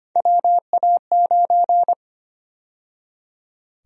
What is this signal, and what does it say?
Morse code, heard as a keyed tone, about 25 wpm, 700 Hz. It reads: WA9
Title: USA Call Sign Prefixes - 25wpm